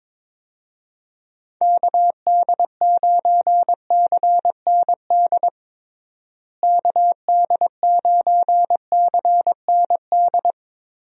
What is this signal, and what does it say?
Morse code, heard as a keyed tone, about 22 wpm, 700 Hz. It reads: KD9CND KD9CND
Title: USA Call Signs - 6 Characters - Repeated 2x - 22wpm